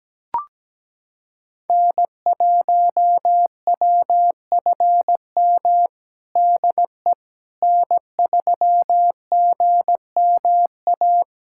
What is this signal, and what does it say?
Morse code, heard as a keyed tone, about 17 wpm, 700 Hz. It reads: N1WFM DE N3GMA
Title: USA Callsign DE Callsign - 5 Characters - 17wpm